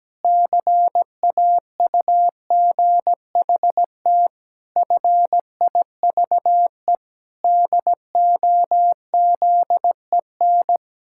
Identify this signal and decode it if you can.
Morse code, heard as a keyed tone, about 17 wpm, 700 Hz. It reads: CAUGHT FIVE DOZEN
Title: Pangram Challenge v2 - 17wpm